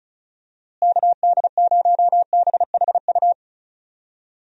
Morse code, heard as a keyed tone, about 34 wpm, 700 Hz. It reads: KD0BHU
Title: USA Call Signs - 6 Characters - Repeated 2x - 35wpm